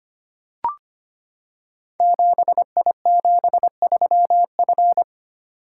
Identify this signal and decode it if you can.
Morse code, heard as a keyed tone, about 25 wpm, 700 Hz. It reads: 7I73F
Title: Code Groups - 5 Letter:Numbers - 25wpm